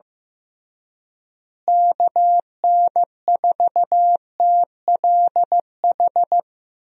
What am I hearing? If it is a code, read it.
Morse code, heard as a keyed tone, about 15 wpm, 700 Hz. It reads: KN4TLH
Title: USA Call Signs - 6 Characters - Repeated 3x - 15wpm